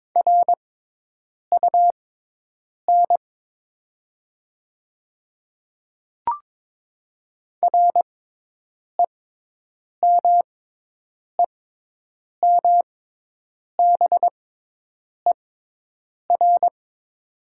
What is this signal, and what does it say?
Morse code, heard as a keyed tone, about 22 wpm, spaced as for 8 wpm, 700 Hz. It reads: RUN REMEMBER
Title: 6x Character Spacing - Top 500 Words - 22wpm